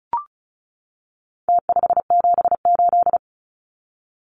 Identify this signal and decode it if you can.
Morse code, heard as a keyed tone, about 34 wpm, 700 Hz. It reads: T578
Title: Numbers - Contest Style - 4-Digit 35wpm